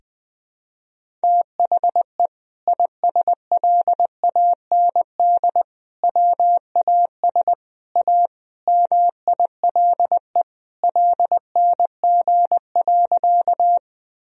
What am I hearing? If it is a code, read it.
Morse code, heard as a keyed tone, about 20 wpm, 700 Hz. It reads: THE ISLAND WAS A MILE LNG.
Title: Sentences from Top 500 Words and Abbreviations 20wpm